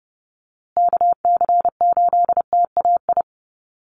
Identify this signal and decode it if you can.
Morse code, heard as a keyed tone, about 30 wpm, 700 Hz. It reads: KC8TAI
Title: USA Call Signs Repeated 3x 30wpm